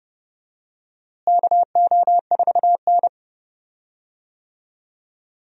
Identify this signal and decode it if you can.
Morse code, heard as a keyed tone, about 30 wpm, 700 Hz. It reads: KO4N
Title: USA Call Signs - 4 Characters - Repeated 3x - 30wpm